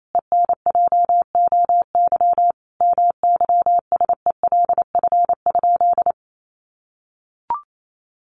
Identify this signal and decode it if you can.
Morse code, heard as a keyed tone, about 28 wpm, 700 Hz. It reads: ENJOY MYSELF?
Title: Sentences from Top 1300 Words 28wpm